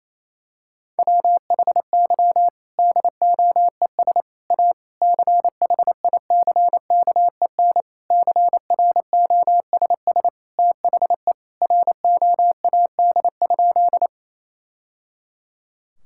Morse code, heard as a keyed tone, about 28 wpm, 700 Hz. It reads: WHY DOES A CHICKEN CROSS THE ROAD?
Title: Sentences from Top 1000 Words - Review Part 5 - 28wpm